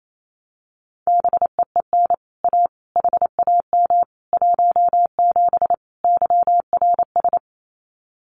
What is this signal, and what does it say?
Morse code, heard as a keyed tone, about 28 wpm, 700 Hz. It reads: BEEN A HAM 17 YRS
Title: Common QSO Phrases Encore 28wpm